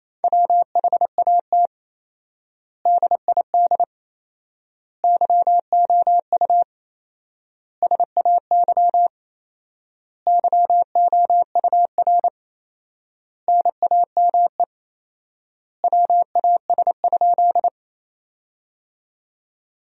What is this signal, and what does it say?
Morse code, heard as a keyed tone, about 28 wpm, 700 Hz. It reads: WHAT DID YOU SAY YOUR NAME WAS?
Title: 4x Word-Spacing - Sentences from Top 200 Words - 28wpm